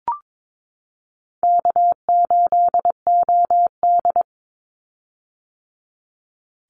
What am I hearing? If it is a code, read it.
Morse code, heard as a keyed tone, about 22 wpm, 700 Hz. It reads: K8OD
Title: Easy Call Signs - 22wpm